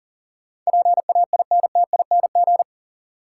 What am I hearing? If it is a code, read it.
Morse code, heard as a keyed tone, about 40 wpm, 700 Hz. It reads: PAINTING
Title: Sentences from Top 1000 Words - Review Part 6 - New Word Drill - 40wpm